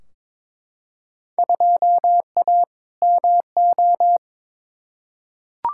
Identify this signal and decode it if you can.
Morse code, heard as a keyed tone, about 22 wpm, 700 Hz. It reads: 2A MO
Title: Field Day Exchanges - 22wpm